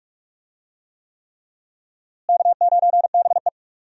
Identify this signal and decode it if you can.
Morse code, heard as a keyed tone, about 42 wpm, 700 Hz. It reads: K9BE
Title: USA Call Signs - 4 Characters - Repeated 3x - 45wpm